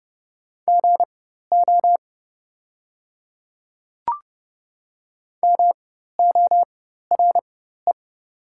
Morse code, heard as a keyed tone, about 30 wpm, spaced as for 14 wpm, 700 Hz. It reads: GO MORE
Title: 4x Character Spacing - Top 100 Words - 30wpm